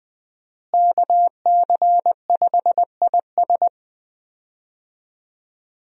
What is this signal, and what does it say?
Morse code, heard as a keyed tone, about 20 wpm, 700 Hz. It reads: KC5IS
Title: USA Call Signs - 5 Characters - Repeated 3x - 20wpm